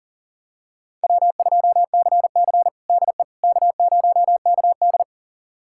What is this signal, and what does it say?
Morse code, heard as a keyed tone, about 40 wpm, 700 Hz. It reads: W2CC DE K0KD
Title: USA Callsign DE Callsign - 4 Characters - 40wpm